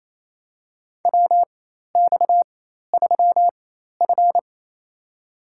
Morse code, heard as a keyed tone, about 28 wpm, spaced as for 13 wpm, 700 Hz. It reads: WX3F
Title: USA Call Signs - 4x Character Spacing - 28wpm